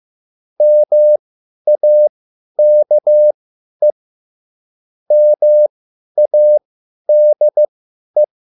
Morse code, heard as a keyed tone, about 15 wpm, spaced as for 10 wpm, 600 Hz. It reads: MAKE MADE
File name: cwbook